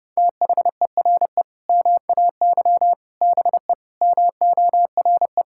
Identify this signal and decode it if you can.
Morse code, heard as a keyed tone, about 28 wpm, 700 Hz. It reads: THERE MAY BE MORE
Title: Sentences from Top 1000 Words - Review Part 6 - 30wpm